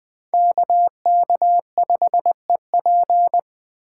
Morse code, heard as a keyed tone, about 20 wpm, 700 Hz. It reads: KK5EP